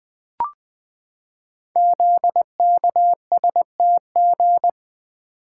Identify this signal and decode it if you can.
Morse code, heard as a keyed tone, about 20 wpm, 700 Hz. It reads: ZKSTG
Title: Code Groups - 5 Letters - 20wpm